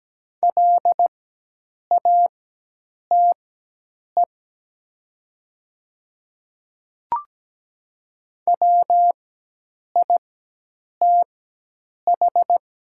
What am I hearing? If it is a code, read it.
Morse code, heard as a keyed tone, about 17 wpm, spaced as for 8 wpm, 700 Hz. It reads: LATE WITH
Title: 4x Character Spacing - Top 300 Words - 17wpm